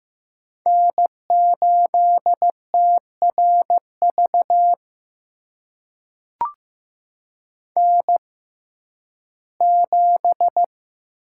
Morse code, heard as a keyed tone, about 15 wpm, spaced as for 11 wpm, 700 Hz. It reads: N8TRV N7
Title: USA Call Signs - 5 Characters - 6x Character Spacing - 15wpm